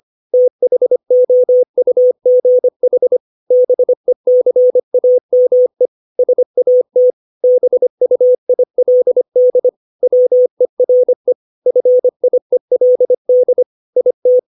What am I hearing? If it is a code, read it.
Morse code, heard as a keyed tone, about 25 wpm, 500 Hz. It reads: THOUGH BECAME SAT BUILD WERE FIELD IT
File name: 632 words / 632 Words 25WPM